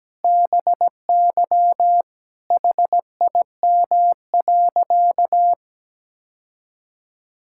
Morse code, heard as a keyed tone, about 17 wpm, 700 Hz. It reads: BY HIM.
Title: Sentences from Top 1000 Words - Review Part 6 - 17wpm